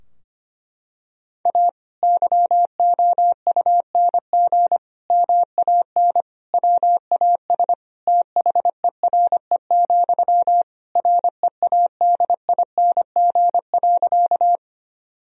Morse code, heard as a keyed tone, about 25 wpm, 700 Hz. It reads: A YOUNG MAN WAS THERE, READING.
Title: Sentences from Top 1000 Words - Review Part 3 - 25wpm